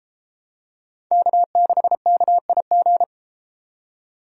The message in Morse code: K6KIG

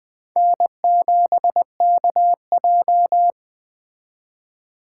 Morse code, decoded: N7KJ

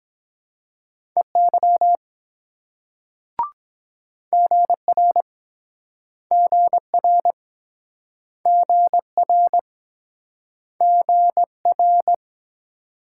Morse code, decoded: EY GR GR GR GR

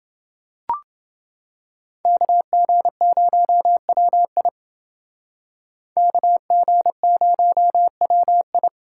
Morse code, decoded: KG0WI KG0WI

